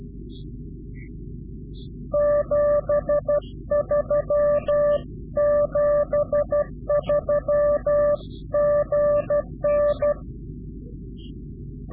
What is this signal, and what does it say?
Morse code, heard as a keyed tone, about 12 wpm, 590 Hz. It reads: 73 73 GN
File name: CW-QSO / Directional Call / Directional QSO